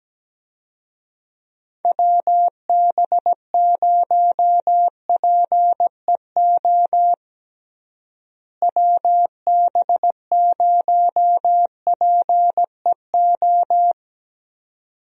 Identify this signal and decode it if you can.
Morse code, heard as a keyed tone, about 17 wpm, 700 Hz. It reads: WB0PEO WB0PEO
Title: USA Call Signs - 6 Characters - Repeated 2x - 17wpm